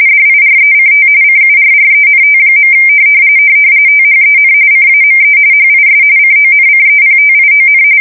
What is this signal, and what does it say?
RTTY, about 45.45 baud. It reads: DIRECT CURRENT. THEN CAME HIS CONCEPTION